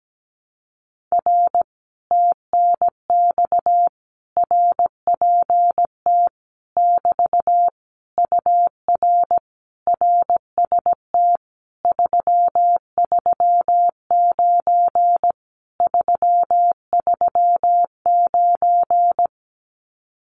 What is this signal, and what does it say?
Morse code, heard as a keyed tone, about 17 wpm, 700 Hz. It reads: R TNX RPT <BT> UR RST 339 339
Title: Common QSO Phrases Encore 17wpm